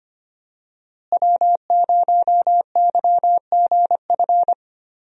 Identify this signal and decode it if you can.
Morse code, heard as a keyed tone, about 25 wpm, 700 Hz. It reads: W0YGF
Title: USA Call Signs - 5 Characters - Repeated 2x - 25wpm